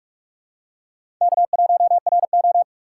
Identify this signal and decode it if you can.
Morse code, heard as a keyed tone, about 44 wpm, 700 Hz. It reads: K1RO
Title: K1USN SST QSO Practice - 45wpm